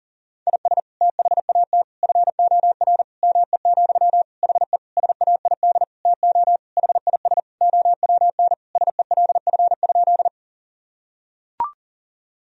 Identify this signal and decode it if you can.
Morse code, heard as a keyed tone, about 40 wpm, 700 Hz. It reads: IS THAT FOR ME, HE SAID TO HIS OWN SELF?
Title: Sentences from Top 1000 Words - Review Part 1 - 40wpm